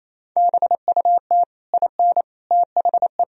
Morse code, heard as a keyed tone, about 28 wpm, 700 Hz. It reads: BUT IN THE
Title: Sentences from Top 1000 Words - Review Part 3 - 28wpm